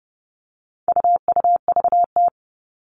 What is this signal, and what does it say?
Morse code, heard as a keyed tone, about 29 wpm, 700 Hz. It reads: UUVT
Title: All Cut Numbers - 4-Digit 30wpm